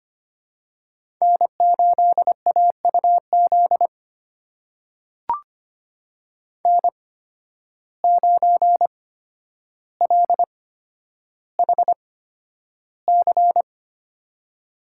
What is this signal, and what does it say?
Morse code, heard as a keyed tone, about 25 wpm, spaced as for 11 wpm, 700 Hz. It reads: N8AUZ N9LHC
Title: USA Call Signs - 5 Characters - 8x Character Spacing - 25wpm